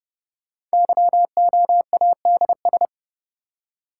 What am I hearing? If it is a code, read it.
Morse code, heard as a keyed tone, about 30 wpm, 700 Hz. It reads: YOADS
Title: Code Groups - 5 Letter:Numbers - 30wpm